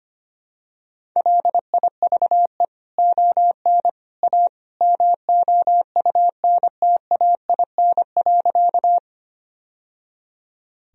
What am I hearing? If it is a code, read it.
Morse code, heard as a keyed tone, about 25 wpm, 700 Hz. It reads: LIVE ON A MOUNTAIN.
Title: Sentences from Top 1000 Words - Review Part 1 - 25wpm